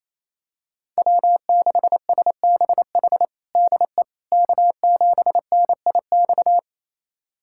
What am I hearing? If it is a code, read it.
Morse code, heard as a keyed tone, about 28 wpm, 700 Hz. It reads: W6SBH DE K7NIX
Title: USA Callsign DE Callsign 28wpm